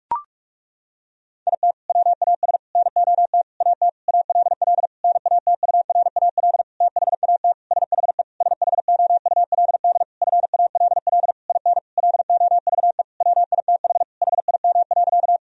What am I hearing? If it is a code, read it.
Morse code, heard as a keyed tone, about 46 wpm, 700 Hz. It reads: IT WAS NOT AT ALL NATURAL THAT SHE SHOULD FALL IN LOVE WITH HIM.
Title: Sentences from Top 1000 Words - Review Part 4 - 45wpm